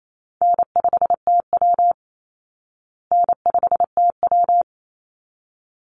Morse code, heard as a keyed tone, about 28 wpm, 700 Hz. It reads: N5TW N5TW
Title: Easy Call Signs Repeated 2x - 28wpm